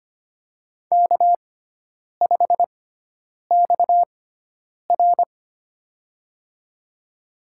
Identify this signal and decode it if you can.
Morse code, heard as a keyed tone, about 25 wpm, spaced as for 9 wpm, 700 Hz. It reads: K5XR
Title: USA Call Signs - 4 Characters - 6x Character Spacing - 25wpm